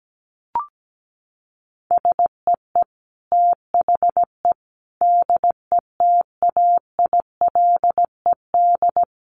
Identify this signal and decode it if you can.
Morse code, heard as a keyed tone, about 17 wpm, 700 Hz. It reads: SEE THE DETAILED
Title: Sentences from Top 1500 Words 17wpm